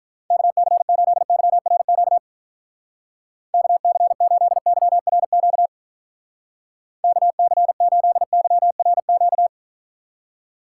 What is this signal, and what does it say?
Morse code, heard as a keyed tone, about 48 wpm, 700 Hz. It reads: KC8YRQ KC8YRQ KC8YRQ